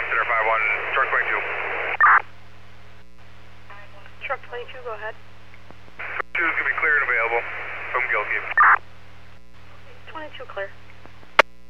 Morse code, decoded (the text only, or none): none